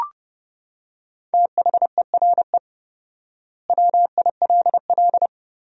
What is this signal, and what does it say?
Morse code, heard as a keyed tone, about 30 wpm, 700 Hz. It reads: THERE WILL
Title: 4x Word-Spacing - Sentences from Top 200 Words - 30wpm